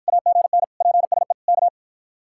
Morse code, heard as a keyed tone, about 53 wpm, 700 Hz. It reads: AGN PSE K